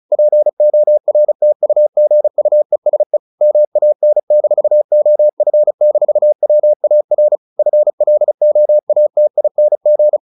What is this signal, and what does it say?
Morse code, heard as a keyed tone, about 35 wpm, 600 Hz. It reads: PORTUGUESE MAN-OF-WAR FLOATING
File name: the old man and the sea / the old man and the sea 1 35wpm